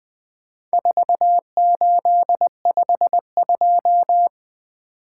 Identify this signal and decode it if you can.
Morse code, heard as a keyed tone, about 21 wpm, 700 Hz. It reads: 4852
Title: Speed-Racing - Four Letter-Number Combinations - 20wpm